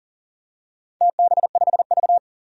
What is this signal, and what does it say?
Morse code, heard as a keyed tone, about 40 wpm, 700 Hz. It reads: TB5V
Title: DXCC-100 Call Signs - 4 Characters - 40wpm